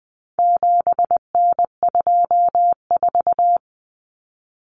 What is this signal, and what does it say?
Morse code, heard as a keyed tone, about 20 wpm, 700 Hz. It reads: 7N24
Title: Common Cut Numbers - 4-Digit 20wpm